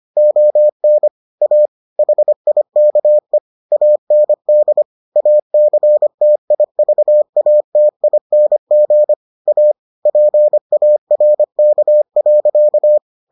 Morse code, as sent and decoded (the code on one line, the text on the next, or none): --- -. / .- / .... .. -.- . / .- -. -.. / .- -.-. - .. ...- .- - .. -. --. / .- / .--. .- .-. -.- .-.-.-
ON A HIKE AND ACTIVATING A PARK.